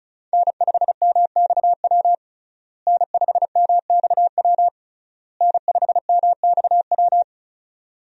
Morse code, heard as a keyed tone, about 36 wpm, 700 Hz. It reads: N5MXW N5MXW N5MXW
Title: USA Call Signs - 5 Characters - Repeated 3x - 35wpm